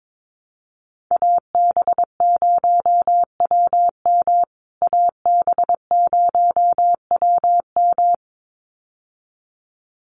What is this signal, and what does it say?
Morse code, heard as a keyed tone, about 22 wpm, 700 Hz. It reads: AB0WM AB0WM
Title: USA Call Signs Repeated 2x 22wpm